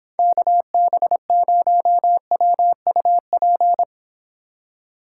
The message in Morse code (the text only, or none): KB0WUP